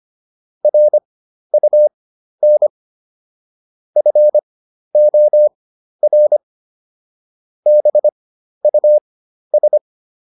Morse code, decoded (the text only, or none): RUN FOR BUS